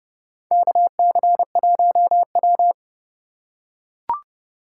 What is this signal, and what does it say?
Morse code, heard as a keyed tone, about 32 wpm, 700 Hz. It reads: KC1W
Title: USA Call Signs - 4 Characters - 2x Character Spacing - 30wpm